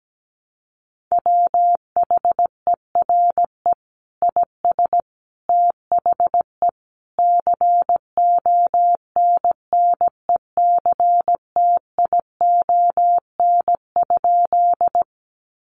Morse code, heard as a keyed tone, about 17 wpm, 700 Hz. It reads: WHERE IS THE CONNECTION?
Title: Sentences from Top 1300 Words 17wpm